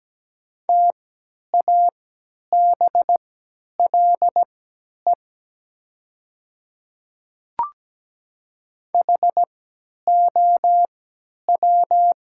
Morse code, heard as a keyed tone, about 17 wpm, spaced as for 10 wpm, 700 Hz. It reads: TABLE HOW
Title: 3x Character Spacing - Top 500 Words - 17wpm